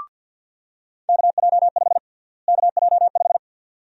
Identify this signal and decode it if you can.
Morse code, heard as a keyed tone, about 53 wpm, 700 Hz. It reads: KJ5 KJ5
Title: USA Call Sign Prefixes - Repeated 2x - 50wpm